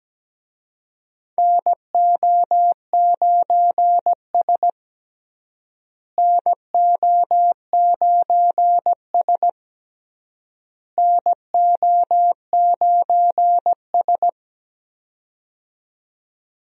Morse code, heard as a keyed tone, about 17 wpm, 700 Hz. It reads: NO9S NO9S NO9S
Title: USA Call Signs - 4 Characters - Repeated 3x - 17wpm